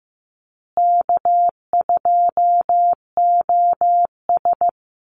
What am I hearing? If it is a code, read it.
Morse code, heard as a keyed tone, about 15 wpm, 700 Hz. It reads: K2OS
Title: Call Signs Repeated 3x - 15wpm